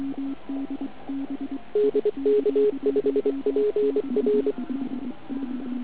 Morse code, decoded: DK5PR